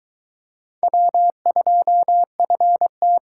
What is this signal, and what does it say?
Morse code, heard as a keyed tone, about 22 wpm, 700 Hz. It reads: W2FT